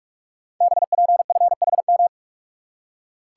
Morse code, heard as a keyed tone, about 44 wpm, 700 Hz. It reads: BPFHM
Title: Code Groups - 5 Letters - 45wpm